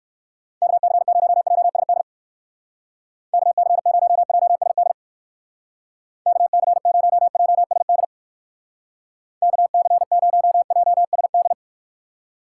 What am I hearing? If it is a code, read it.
Morse code, heard as a keyed tone, about 58 wpm, 700 Hz. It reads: KC0JSD KC0JSD KC0JSD KC0JSD